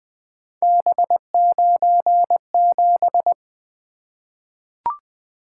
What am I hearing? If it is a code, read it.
Morse code, heard as a keyed tone, about 20 wpm, 700 Hz. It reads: B97
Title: Three Letter-Number Combinations - 20wpm